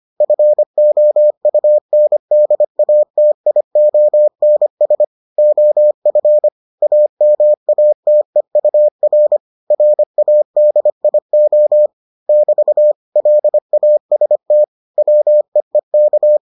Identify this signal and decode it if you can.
Morse code, heard as a keyed tone, about 25 wpm, 600 Hz. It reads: FOUNDATIONS OF AMATEUR RADIO <BT> LAST WEEK